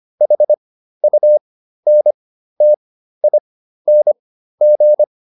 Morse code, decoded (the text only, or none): HUNTING